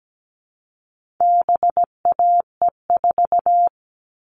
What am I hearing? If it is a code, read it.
Morse code, heard as a keyed tone, about 17 wpm, 700 Hz. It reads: BAE4
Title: All Cut Numbers - 4-Digit 17wpm